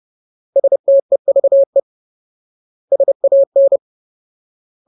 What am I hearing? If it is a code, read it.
Morse code, heard as a keyed tone, about 30 wpm, 550 Hz. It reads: STEVE SAN
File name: Ultimate Word List / 30WPM Ultimate